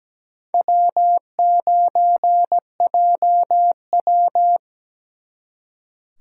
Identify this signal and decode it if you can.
Morse code, heard as a keyed tone, about 20 wpm, 700 Hz. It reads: W9JW
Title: Speed Racing - USA 1x2 Call Signs 17wpm